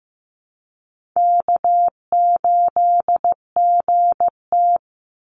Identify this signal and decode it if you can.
Morse code, heard as a keyed tone, about 15 wpm, 700 Hz. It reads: K8GT
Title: Call Signs Repeated 2x - 15wpm